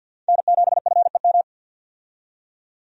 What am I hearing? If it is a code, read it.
Morse code, heard as a keyed tone, about 51 wpm, 700 Hz. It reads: N7FEM
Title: USA Call Signs - 5 Characters - 50wpm